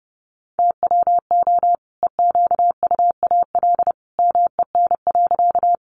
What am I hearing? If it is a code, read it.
Morse code, heard as a keyed tone, about 30 wpm, 700 Hz. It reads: TWO EQUAL MEN.
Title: Sentences from Top 1400 Words 30wpm